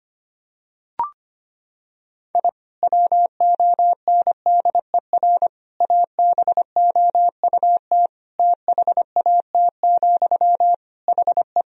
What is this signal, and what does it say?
Morse code, heard as a keyed tone, about 25 wpm, 700 Hz. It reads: I WONDER ABOUT THAT, HE
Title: Sentences from Top 1000 Words - Review Part 3 - 25wpm